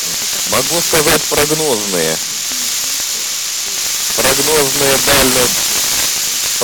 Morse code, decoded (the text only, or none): none